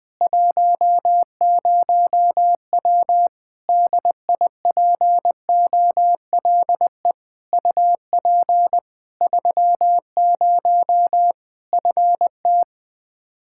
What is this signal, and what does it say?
Morse code, heard as a keyed tone, about 20 wpm, 700 Hz. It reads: 10W DIPOLE UP 30 FT
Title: Practice QSO - Exchange 2 - Info - 20wpm